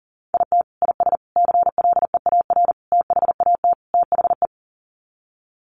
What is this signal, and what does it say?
Morse code, heard as a keyed tone, about 40 wpm, 700 Hz. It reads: IT IS CLEAR THAT THE
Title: Sets of 5 Words Encore 40wpm